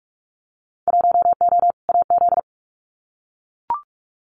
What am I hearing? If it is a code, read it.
Morse code, heard as a keyed tone, about 44 wpm, 700 Hz. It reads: 1O AZ
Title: Winter Field Day Exchanges 45wpm